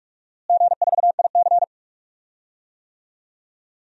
Morse code, heard as a keyed tone, about 41 wpm, 700 Hz. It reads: G4IC
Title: DXCC-100 Call Signs - 4 Characters - Repeated 3x - 45wpm